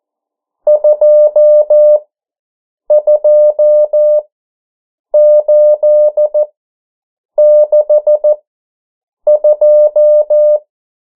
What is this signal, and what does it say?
Morse code, heard as a keyed tone, about 14 wpm, spaced as for 7 wpm, 600 Hz. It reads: 22862
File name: morse mp3 / 09 urok 8